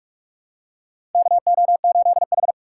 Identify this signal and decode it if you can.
Morse code, heard as a keyed tone, about 44 wpm, 700 Hz. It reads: KO8H
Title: USA Call Signs - 6x Character Spacing - 45wpm